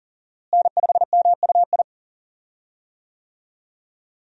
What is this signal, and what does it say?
Morse code, heard as a keyed tone, about 37 wpm, 700 Hz. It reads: N5MUI
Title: USA Call Signs - 5 Characters - Repeated 2x - 40wpm